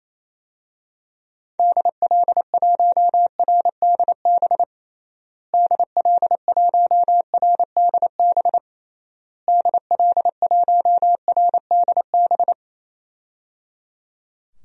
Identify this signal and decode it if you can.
Morse code, heard as a keyed tone, about 28 wpm, 700 Hz. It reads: DL1RDB DL1RDB DL1RDB
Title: DXCC-100 Call Signs - 6 Characters - Repeated 3x - 28wpm